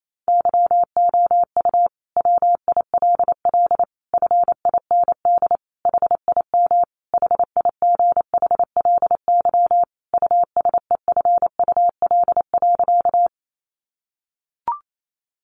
Morse code, heard as a keyed tone, about 28 wpm, 700 Hz. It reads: YOU WILL FIND HIM HIGHLY USEFUL.
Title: Sentences from Top 1100 Words 28wpm